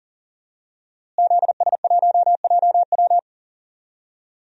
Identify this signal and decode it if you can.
Morse code, heard as a keyed tone, about 41 wpm, 700 Hz. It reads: ZS1JW
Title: DXCC-100 Call Signs - 5 Characters - Repeated 3x - 40wpm